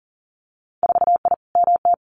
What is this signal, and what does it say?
Morse code, heard as a keyed tone, about 40 wpm, 700 Hz. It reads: 4I MT
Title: Winter Field Day Exchanges Encore 40wpm